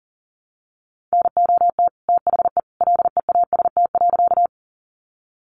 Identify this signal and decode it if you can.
Morse code, heard as a keyed tone, about 40 wpm, 700 Hz. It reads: NOT THE LEAST.
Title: Sentences from Top 700 Words 40wpm